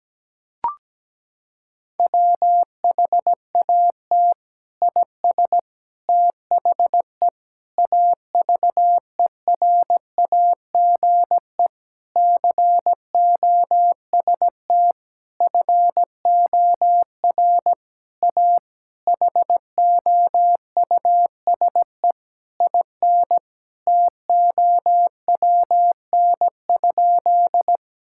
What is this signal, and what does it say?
Morse code, heard as a keyed tone, about 17 wpm, 700 Hz. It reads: WHAT IS THE AVERAGE COST FOR A HOUSE IN TOWN?
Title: Sentences from Top 1000 Words - Review Part 6 - 17wpm